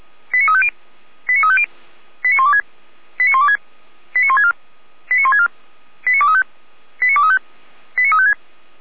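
Five-tone selective call (ZVEI), heard as quick runs of five tones, tones of about 70 ms. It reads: 89350 89350 89126 89126 89164 89164 89235 89235 89356
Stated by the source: ZVEI five-tone call received on SDR.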